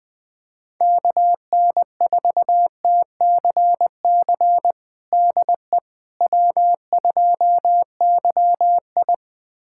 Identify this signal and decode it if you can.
Morse code, heard as a keyed tone, about 20 wpm, 700 Hz. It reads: KN4TCC DE W2YI
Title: USA Callsign DE Callsign 20wpm